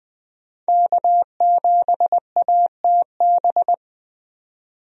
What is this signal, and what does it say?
Morse code, heard as a keyed tone, about 20 wpm, 700 Hz. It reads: K7ATB
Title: USA Call Signs - 5 Characters - 20wpm